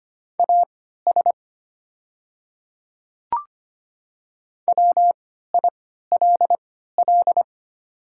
Morse code, heard as a keyed tone, about 25 wpm, spaced as for 14 wpm, 700 Hz. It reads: AS WILL
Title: 3x Character Spacing - Top 100 Words - 25wpm